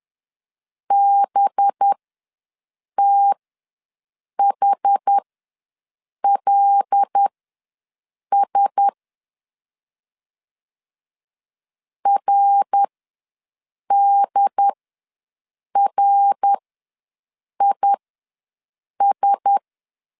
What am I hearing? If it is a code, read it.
Morse code, heard as a keyed tone, about 10 wpm, spaced as for 6 wpm, 800 Hz. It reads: BTHLS RDRIS
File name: LEZIONI / 5 LBVC / 5 LB ws 10wpm 3616